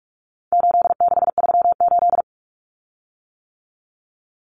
Morse code, heard as a keyed tone, about 44 wpm, 700 Hz. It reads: 8638
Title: Common Cut Numbers - 4-Digit 45wpm